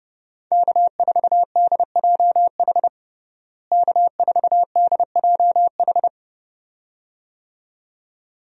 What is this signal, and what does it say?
Morse code, heard as a keyed tone, about 32 wpm, 700 Hz. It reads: K4DJH K4DJH
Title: USA Call Signs - 5 Characters - Repeated 2x - 30wpm